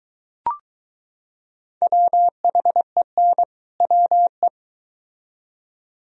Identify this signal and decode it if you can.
Morse code, heard as a keyed tone, about 23 wpm, 700 Hz. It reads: WHEN WE